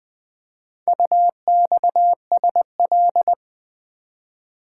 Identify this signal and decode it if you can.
Morse code, heard as a keyed tone, about 20 wpm, 700 Hz. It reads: UXSL
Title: Four Letter-Number Combinations - 20wpm